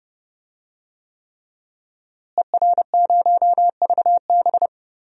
Morse code, heard as a keyed tone, about 30 wpm, 700 Hz. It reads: ER0VB
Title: DXCC-100 Call Signs - 5 Characters - Repeated 3x - 30wpm